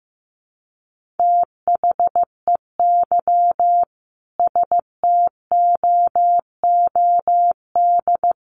Morse code, heard as a keyed tone, about 15 wpm, 700 Hz. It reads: THEY STOOD
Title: Sentences from Top 1400 Words 15wpm